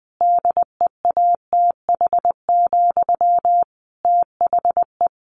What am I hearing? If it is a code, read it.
Morse code, heard as a keyed tone, about 20 wpm, 700 Hz. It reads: DEATH, THE
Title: A Sound Of Thunder 20wpm